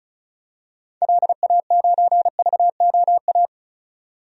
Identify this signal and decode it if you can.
Morse code, heard as a keyed tone, about 34 wpm, 700 Hz. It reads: LA9VOA
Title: DXCC-100 Call Signs - 6 Characters - 35wpm